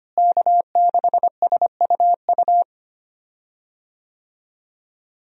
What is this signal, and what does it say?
Morse code, heard as a keyed tone, about 25 wpm, 700 Hz. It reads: K6SUU
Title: USA Call Signs - 5 Characters - Repeated 3x - 25wpm